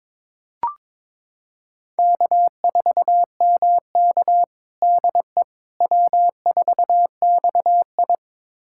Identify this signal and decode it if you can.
Morse code, heard as a keyed tone, about 22 wpm, 700 Hz. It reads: K4MK DE W4XI
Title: USA Callsign DE Callsign - 4 Characters - 22wpm